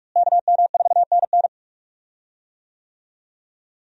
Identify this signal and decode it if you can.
Morse code, heard as a keyed tone, about 47 wpm, 700 Hz. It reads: KM4NN
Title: USA Call Signs - 5 Characters - Repeated 2x - 45wpm